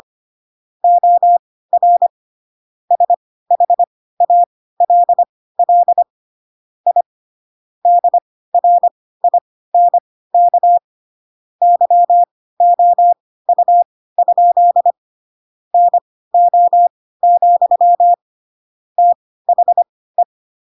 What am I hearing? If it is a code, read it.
Morse code, heard as a keyed tone, about 25 wpm, spaced as for 16 wpm, 700 Hz. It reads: OR SHALL I DRINK YOU? NO, THE
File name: Pinocchio / Pinocchio ch5 10 25 ik1hge